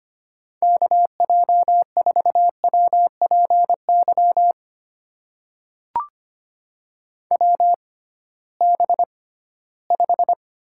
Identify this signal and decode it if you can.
Morse code, heard as a keyed tone, about 25 wpm, spaced as for 16 wpm, 700 Hz. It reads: KJ4WPY WB5
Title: USA Call Signs - 6 Characters - 6x Character Spacing - 25wpm